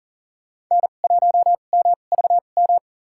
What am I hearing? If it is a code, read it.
Morse code, heard as a keyed tone, about 40 wpm, spaced as for 29 wpm, 690 Hz. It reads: N1MVM
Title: USA Call Signs - 2x Character Spacing - 40wpm